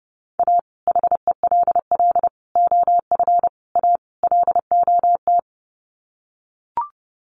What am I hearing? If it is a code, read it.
Morse code, heard as a keyed tone, about 30 wpm, 700 Hz. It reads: A HELL OF A LOT
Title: Sets of 5 Words Encore 30wpm